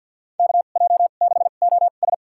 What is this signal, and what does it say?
Morse code, heard as a keyed tone, about 48 wpm, spaced as for 35 wpm, 700 Hz. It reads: KJ6OS
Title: USA Call Signs - 5 Characters - 2x Character Spacing - 50wpm